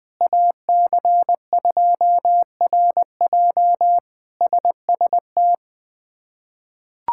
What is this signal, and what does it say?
Morse code, heard as a keyed tone, about 20 wpm, 700 Hz. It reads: AC2RJ SST
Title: K1USN SST QSO Practice - 20wpm